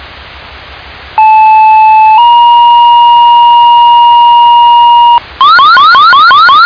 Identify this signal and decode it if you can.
Two-tone page, heard as a one-second tone, then a three-second tone, about 870 then 950 Hz. It is heard from 1.2 to 5.2 s.